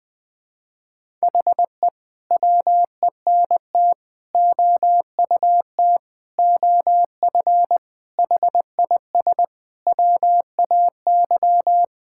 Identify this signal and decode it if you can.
Morse code, heard as a keyed tone, about 20 wpm, 700 Hz. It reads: HE WENT OUT OF HIS WAY
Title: Sentences with 6-Word Phrases - 20wpm